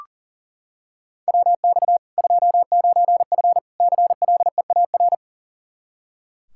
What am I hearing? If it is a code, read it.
Morse code, heard as a keyed tone, about 40 wpm, 700 Hz. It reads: WX 29F CLEAR
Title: Practice QSO - Exchange 2 - Info - 40wpm